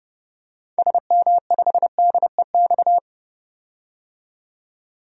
Morse code, heard as a keyed tone, about 31 wpm, 700 Hz. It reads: SM5DEX
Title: DXCC-100 Call Signs - 6 Characters - 30wpm